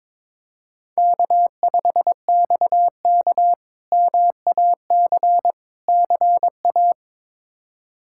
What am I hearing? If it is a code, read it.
Morse code, heard as a keyed tone, about 22 wpm, 700 Hz. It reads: K5XK MAC CA
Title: K1USN SST QSO Practice - 22wpm